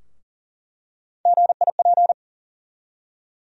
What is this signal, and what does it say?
Morse code, heard as a keyed tone, about 40 wpm, 700 Hz. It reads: GIP
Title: USA Call Sign Postfixes - 3 Letters 40wpm